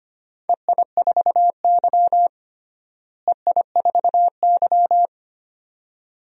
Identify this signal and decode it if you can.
Morse code, heard as a keyed tone, about 25 wpm, 700 Hz. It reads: EI4Y EI4Y
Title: DXCC-100 Call Signs - 4 Characters - Repeated 2x - 25wpm